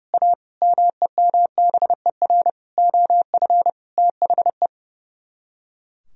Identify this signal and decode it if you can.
Morse code, heard as a keyed tone, about 37 wpm, 700 Hz. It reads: A MEMBER OF THE
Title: Speed-Racing - Sets of 5 Words 30wpm